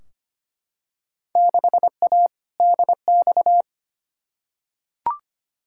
Morse code, heard as a keyed tone, about 25 wpm, 700 Hz. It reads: 6A DX
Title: Field Day Exchanges Encore - 25wpm